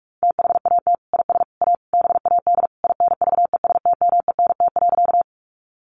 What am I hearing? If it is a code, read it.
Morse code, heard as a keyed tone, about 44 wpm, 700 Hz. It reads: THAT IS A BAD INVESTMENT.
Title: Sentences from Top 700 Words 45wpm